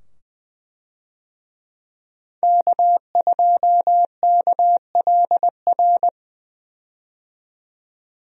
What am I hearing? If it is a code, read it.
Morse code, heard as a keyed tone, about 20 wpm, 700 Hz. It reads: K2KLR
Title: USA Call Signs - 5 Characters - 20wpm